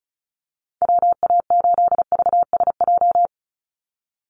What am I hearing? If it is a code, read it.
Morse code, heard as a keyed tone, about 34 wpm, 700 Hz. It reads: WA8VSJ
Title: Call Signs - 35wpm